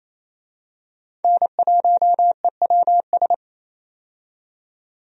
Morse code, heard as a keyed tone, about 27 wpm, 700 Hz. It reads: N1EWS
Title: USA Call Signs - 5 Characters - 28wpm